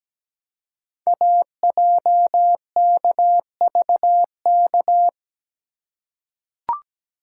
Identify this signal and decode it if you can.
Morse code, heard as a keyed tone, about 17 wpm, 700 Hz. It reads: AJKVK
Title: Code Groups - 5 Letters - 17wpm